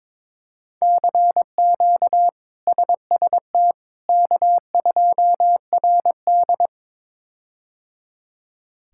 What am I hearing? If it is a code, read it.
Morse code, heard as a keyed tone, about 22 wpm, 700 Hz. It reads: CQ SST K2RD
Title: K1USN SST QSO Practice - 22wpm